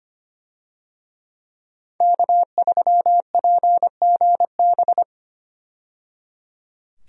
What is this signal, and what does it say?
Morse code, heard as a keyed tone, about 25 wpm, 700 Hz. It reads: K3PGB
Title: USA Call Signs - 5 Characters - 25wpm